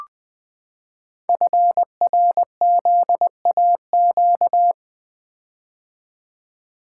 Morse code, heard as a keyed tone, about 21 wpm, 700 Hz. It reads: FRZAQ
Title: Code Groups - 5 Letters - 20wpm